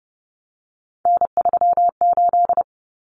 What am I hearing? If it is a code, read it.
Morse code, heard as a keyed tone, about 30 wpm, 700 Hz. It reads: N38
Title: Common Cut Numbers - 3-Digit 30wpm